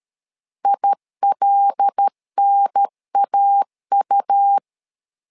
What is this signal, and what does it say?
Morse code, heard as a keyed tone, about 13 wpm, 800 Hz. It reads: ILNAU
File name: LEZIONI / 5 LBVC / 5 LBV 12wpm 1855